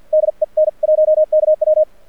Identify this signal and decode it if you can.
Morse code, heard as a keyed tone, about 50 wpm, spaced as for 42 wpm, 620 Hz. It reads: DE N1KW
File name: archived / CRSnet